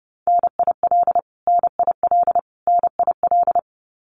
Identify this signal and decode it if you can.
Morse code, heard as a keyed tone, about 30 wpm, 700 Hz. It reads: NIL NIL NIL
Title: Abbreviations Repeated 3x - 30wpm